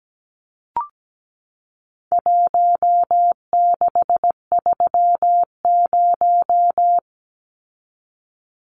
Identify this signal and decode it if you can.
Morse code, heard as a keyed tone, about 17 wpm, 700 Hz. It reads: 1630
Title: Numbers - Contest Style - 4-Digit 17wpm